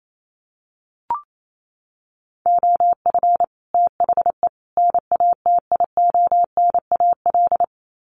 Morse code, heard as a keyed tone, about 28 wpm, 700 Hz. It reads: OF THE NATIONAL
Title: Sets of 3 Words 28wpm